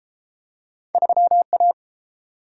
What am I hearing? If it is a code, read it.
Morse code, heard as a keyed tone, about 31 wpm, 700 Hz. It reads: 3A